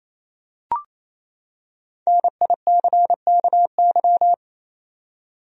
Morse code, heard as a keyed tone, about 28 wpm, 700 Hz. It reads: NICKY
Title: USA Common Names - 28wpm